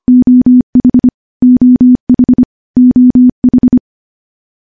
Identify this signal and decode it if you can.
Morse code, heard as a keyed tone, about 25 wpm, 250 Hz. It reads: OH OH OH
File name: morse / morse-training-25WPM